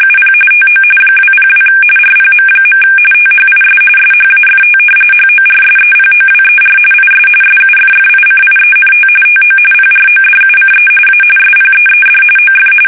RTTY, about 45.45 baud. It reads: UP TO 202 RPM. THIS SHAFT HORSEPOWER IS SUFFICIENT TO PRODUCE A SPEED OF
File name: fsk850 / battleshipnj